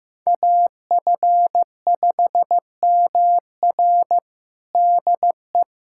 Morse code, heard as a keyed tone, about 16 wpm, 700 Hz. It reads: AF5MR DE
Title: USA Callsign DE Callsign - 5 Characters - 15wpm